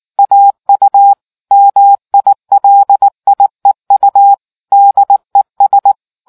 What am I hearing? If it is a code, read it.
Morse code, heard as a keyed tone, about 19 wpm, 800 Hz. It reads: AU MILIEU DES